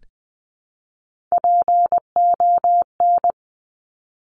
Morse code, heard as a keyed tone, about 20 wpm, 700 Hz. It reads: PON
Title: Top Three Letter Combinations in Words 20wpm